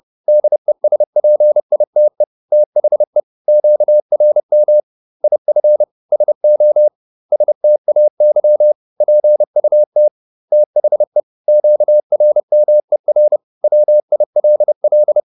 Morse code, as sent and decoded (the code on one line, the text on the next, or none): -.. ... .--. .. - . / - .... / --.- .-. -- / .. ..-. / ... --- / ... - .- -.-- / .--. ..- - / - .... / --.- .-. -- . .-. / .-- .. .-.. .-..
DESPITE THE QRM IF SO STAY PUT THE QRMER WILL